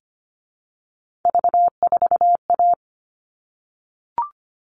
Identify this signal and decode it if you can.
Morse code, heard as a keyed tone, about 25 wpm, 700 Hz. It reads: V4A
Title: All Cut Numbers - 3-Digit 25wpm